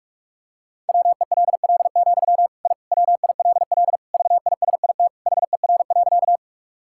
Morse code, heard as a keyed tone, about 41 wpm, 690 Hz. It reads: WELL, I WILL VISIT HER.
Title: Sentences from Top 1000 Words - Review Part 6 - 45wpm